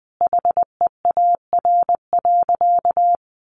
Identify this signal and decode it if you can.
Morse code, heard as a keyed tone, about 20 wpm, 700 Hz. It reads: HEAR.
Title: Sentences from Top 600 Words 20wpm